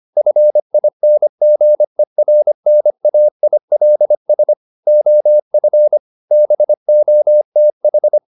Morse code, heard as a keyed tone, about 25 wpm, 600 Hz. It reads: FINGERNAILS OF BOTH